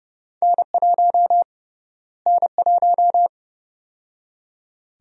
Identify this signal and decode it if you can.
Morse code, heard as a keyed tone, about 29 wpm, 700 Hz. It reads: N1 N1
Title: USA Call Sign Prefixes - Repeated 2x - 30wpm